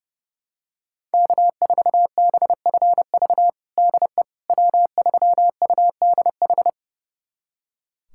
Morse code, heard as a keyed tone, about 30 wpm, 700 Hz. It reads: K4BFV DE W3UDH
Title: USA Callsign DE Callsign - 5 Characters - 30wpm